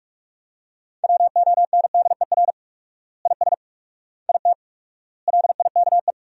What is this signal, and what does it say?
Morse code, heard as a keyed tone, about 44 wpm, 700 Hz. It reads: WONDER IS IT LIKE
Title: 4x Word-Spacing - Sentences from Top 500 Words - 45wpm